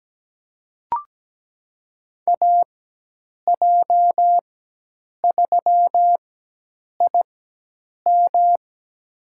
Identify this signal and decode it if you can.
Morse code, heard as a keyed tone, about 17 wpm, spaced as for 8 wpm, 700 Hz. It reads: AJ3IM
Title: USA Call Signs - 5 Characters - 4x Character Spacing - 17wpm